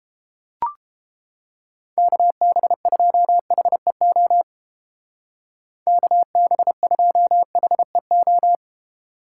Morse code, heard as a keyed tone, about 32 wpm, 700 Hz. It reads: KB2HEO KB2HEO